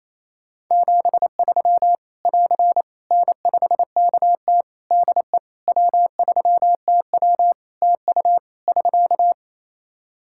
Morse code, heard as a keyed tone, about 28 wpm, 700 Hz. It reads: 73 <AR> N5KT DE W3TW TU <SK>
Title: Practice QSO - Exchange 4 - Ending - 28wpm